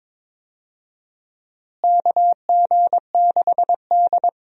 Morse code, heard as a keyed tone, about 22 wpm, 700 Hz. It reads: KG6D
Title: USA Call Signs - 4 Characters - Repeated 2x - 22wpm